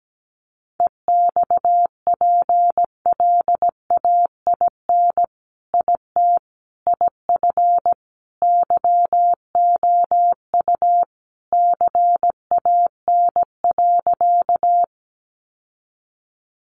Morse code, heard as a keyed tone, about 17 wpm, 700 Hz. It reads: EXPLAIN IT IF YOU CAN.
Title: Sentences from Top 1100 Words 17wpm